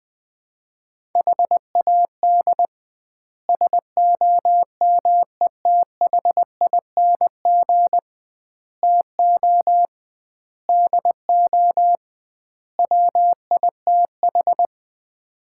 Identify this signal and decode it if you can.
Morse code, heard as a keyed tone, about 20 wpm, 700 Hz. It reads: HAD SOMETHING TO DO WITH
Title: 2x Word-Spacing - Sets of 5 Words - 20wpm